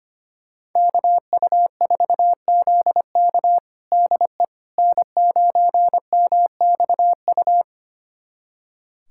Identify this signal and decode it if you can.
Morse code, heard as a keyed tone, about 25 wpm, 700 Hz. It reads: KU4ZK DE N9MXU
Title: USA Callsign DE Callsign - 5 Characters - 25wpm